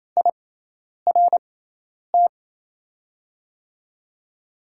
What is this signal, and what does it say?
Morse code, heard as a keyed tone, about 28 wpm, spaced as for 10 wpm, 700 Hz. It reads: IRT
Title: USA Call Signs - 6x Character Spacing - 28wpm